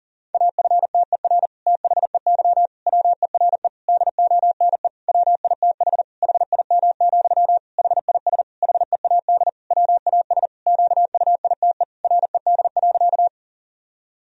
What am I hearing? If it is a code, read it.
Morse code, heard as a keyed tone, about 40 wpm, 700 Hz. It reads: AFTER THEY WERE DONE WITH HIM, HIS HEAD WAS QUITE RED.
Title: Sentences from Top 1000 Words - Review Part 2 - 40wpm